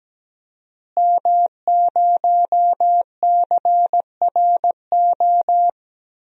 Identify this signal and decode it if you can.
Morse code, heard as a keyed tone, about 17 wpm, 700 Hz. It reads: M0CRO
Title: DXCC-100 Call Signs - 5 Characters - Repeated 2x - 17wpm